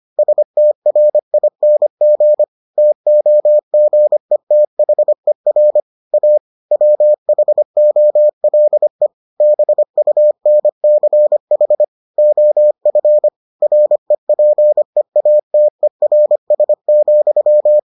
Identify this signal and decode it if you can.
Morse code, heard as a keyed tone, about 25 wpm, 600 Hz. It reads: STRING TOGETHER A WHOLE BUNCH OF REPEATERS,